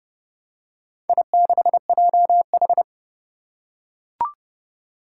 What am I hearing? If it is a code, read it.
Morse code, heard as a keyed tone, about 30 wpm, 700 Hz. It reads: I6JH
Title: Four Challenging Letter-Number Combinations 30wpm